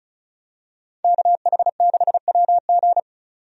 Morse code, heard as a keyed tone, about 36 wpm, 700 Hz. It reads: KH6WG
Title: DXCC-100 Call Signs - 5 Characters - 35wpm